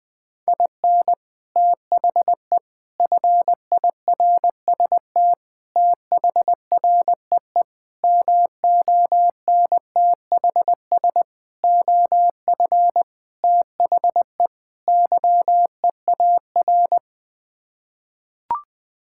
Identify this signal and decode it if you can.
Morse code, heard as a keyed tone, about 20 wpm, 700 Hz. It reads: IN THE FIRST THREE MONTHS OF THE YEAR
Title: Sets of 8 Words - 20wpm